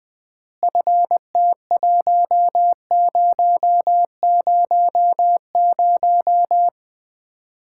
Morse code, heard as a keyed tone, about 20 wpm, 700 Hz. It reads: FT1000
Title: Common QSO Phrases - 3x Word-Spacing - 20wpm